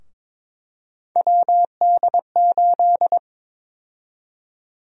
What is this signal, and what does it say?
Morse code, heard as a keyed tone, about 22 wpm, 700 Hz. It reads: WD8
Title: USA Call Sign Prefixes - 22wpm